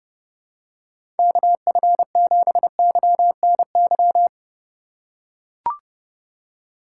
Morse code, heard as a keyed tone, about 30 wpm, 700 Hz. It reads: KF7YNY